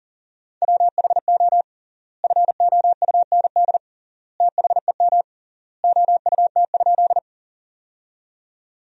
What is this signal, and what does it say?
Morse code, heard as a keyed tone, about 40 wpm, 700 Hz. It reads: WHO FOUND THEM OUT?
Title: 3x Word-Spacing - Sentences from Top 300 Words - 40wpm